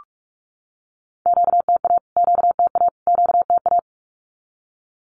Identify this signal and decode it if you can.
Morse code, heard as a keyed tone, about 44 wpm, 700 Hz. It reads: QTA QTA QTA
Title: Q-Codes Repeated 3x 45wpm